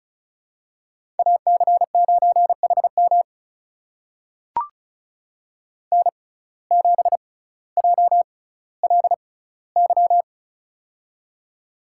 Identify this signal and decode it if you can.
Morse code, heard as a keyed tone, about 36 wpm, spaced as for 18 wpm, 700 Hz. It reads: AC9HM N7JLY
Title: USA Call Signs - 5 Characters - 6x Character Spacing - 35wpm